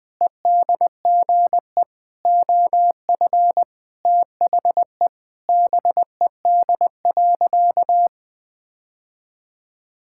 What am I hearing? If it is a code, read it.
Morse code, heard as a keyed tone, about 19 wpm, 700 Hz. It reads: EDGE OF THE BED.
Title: Sentences with 5-Word Phrases - 20wpm